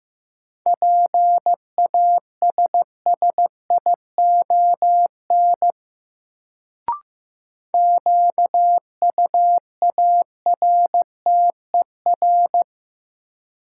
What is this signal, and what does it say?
Morse code, heard as a keyed tone, about 15 wpm, 700 Hz. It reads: PASSION QUARTER
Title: Sentences from Top 1300 Words - New Word Drill Alt - 15wpm